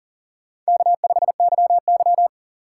Morse code, heard as a keyed tone, about 40 wpm, 700 Hz. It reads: K5YY
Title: USA Call Signs - 4 Characters - 2x Character Spacing - 40wpm